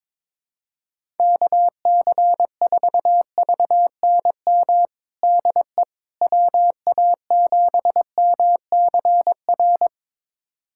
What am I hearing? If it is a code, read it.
Morse code, heard as a keyed tone, about 22 wpm, 700 Hz. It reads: KC4VNM DE WA7MCR
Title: USA Callsign DE Callsign - 6 Characters - 22wpm